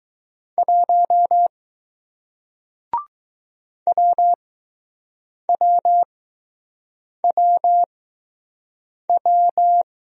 Morse code, heard as a keyed tone, about 20 wpm, spaced as for 6 wpm, 700 Hz. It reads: 1 WWWW